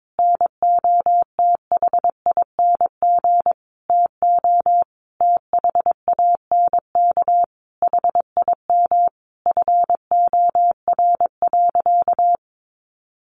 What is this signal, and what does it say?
Morse code, heard as a keyed tone, about 22 wpm, 700 Hz. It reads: NOTHING TO THANK HIM FOR.
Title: Sentences from Top 1000 Words 22wpm